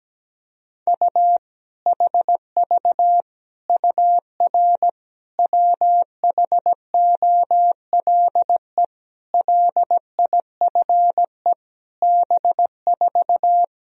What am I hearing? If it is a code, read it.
Morse code, heard as a keyed tone, about 17 wpm, 700 Hz. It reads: U HV UR WHOLE LIFE B4
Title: Sentences from Top 400 Words and Abbreviations 17wpm